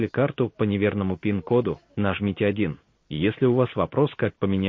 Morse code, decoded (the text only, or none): none